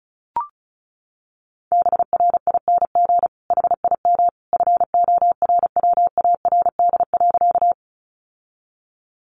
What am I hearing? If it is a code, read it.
Morse code, heard as a keyed tone, about 34 wpm, 700 Hz. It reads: BRING HIM FORWARD.
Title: Sentences from Top 1300 Words 35wpm